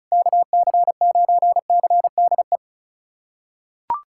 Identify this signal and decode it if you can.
Morse code, heard as a keyed tone, about 36 wpm, 700 Hz. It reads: KC9CDE
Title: USA Callsign DE Callsign 35wpm